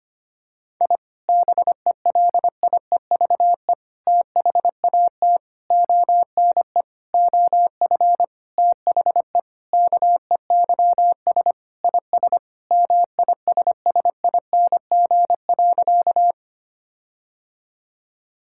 Morse code, heard as a keyed tone, about 25 wpm, 700 Hz. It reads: I BELIEVE THAT ONE OF THE KEYS IS MISSING.
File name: practice / 25wpm / Sentences with 4-Word Phrases - 25wpm